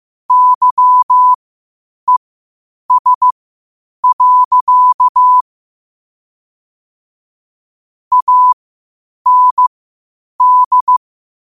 Morse code, heard as a keyed tone, about 15 wpm, spaced as for 8 wpm, 1000 Hz. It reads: YES. AND